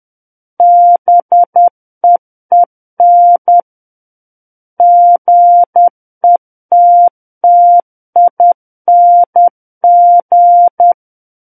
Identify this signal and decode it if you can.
Morse code, heard as a keyed tone, about 10 wpm, 700 Hz. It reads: BEEN GETTING